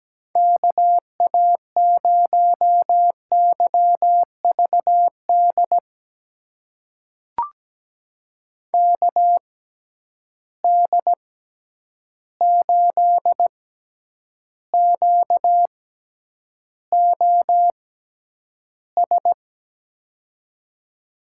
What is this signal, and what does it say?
Morse code, heard as a keyed tone, about 17 wpm, spaced as for 9 wpm, 700 Hz. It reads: KA0YVD KD8QOS
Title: USA Call Signs - 6 Characters - 6x Character Spacing - 17wpm